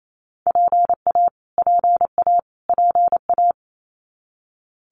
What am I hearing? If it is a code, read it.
Morse code, heard as a keyed tone, about 28 wpm, 700 Hz. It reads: PA PA PA
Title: US States Repeated 3x 28wpm